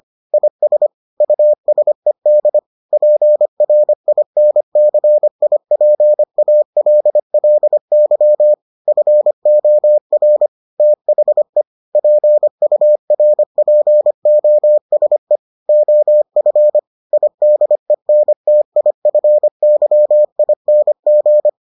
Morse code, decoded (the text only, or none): IS USED PRINCIPALLY FOR THE PURPOSE OF IDENTIFYING